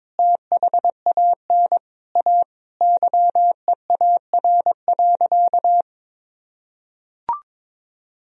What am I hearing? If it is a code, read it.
Morse code, heard as a keyed tone, about 21 wpm, 700 Hz. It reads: THAN A YEAR.
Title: Sentences from Top 1000 Words - Review Part 1 - 22wpm